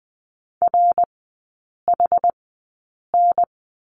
Morse code, heard as a keyed tone, about 20 wpm, spaced as for 8 wpm, 700 Hz. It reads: RHN
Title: 060 Lesson 11e - Letter h - Three Characters 20wpm